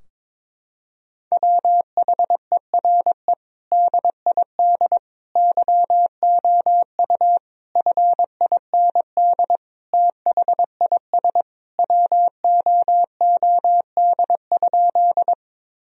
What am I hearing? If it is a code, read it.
Morse code, heard as a keyed tone, about 22 wpm, 700 Hz. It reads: WHERE DID YOU FIND THIS WOOD?
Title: Sentences from Top 1000 Words - Review Part 2 - 22wpm